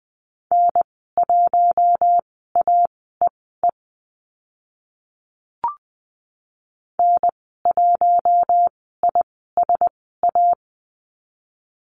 Morse code, heard as a keyed tone, about 20 wpm, spaced as for 15 wpm, 700 Hz. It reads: N1AEE N1ISA
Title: 042 Lesson 8g - Number 1 - Call Signs 20wpm